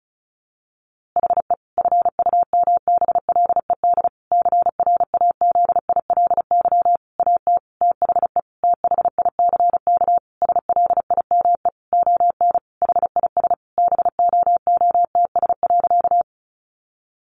Morse code, heard as a keyed tone, about 34 wpm, 700 Hz. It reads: HE FUMBLED CRAZILY AT THE THICK SLIME ON HIS BOOTS.
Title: A Sound Of Thunder 35wpm